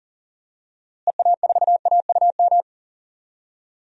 Morse code, heard as a keyed tone, about 37 wpm, 700 Hz. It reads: EA4AUM
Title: DXCC-100 Call Signs - 6 Characters - Repeated 3x - 40wpm